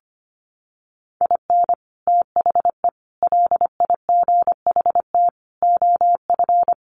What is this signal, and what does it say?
Morse code, heard as a keyed tone, about 25 wpm, 700 Hz. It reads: IN THE LIGHT OF
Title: Sets of 5 Words Encore2 25wpm